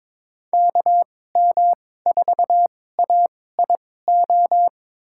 Morse code, heard as a keyed tone, about 21 wpm, spaced as for 16 wpm, 700 Hz. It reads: KM4AIO
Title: USA Call Signs - 6 Characters - 2x Character Spacing - 22wpm